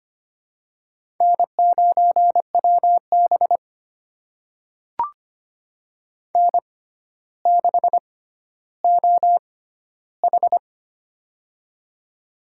Morse code, heard as a keyed tone, about 25 wpm, spaced as for 13 wpm, 700 Hz. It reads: N9WB N6OH
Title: USA Call Signs - 4 Characters - 6x Character Spacing - 25wpm